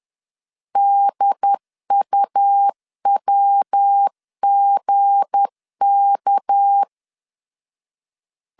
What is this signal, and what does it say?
Morse code, heard as a keyed tone, about 10 wpm, 800 Hz. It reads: DUWGK
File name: CW / DET / 1 9 1500C S A 10wpm 4511